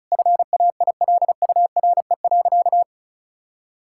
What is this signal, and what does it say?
Morse code, heard as a keyed tone, about 36 wpm, 700 Hz. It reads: FAILURE.
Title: Sentences from Top 1000 Words - Review Part 6 - 35wpm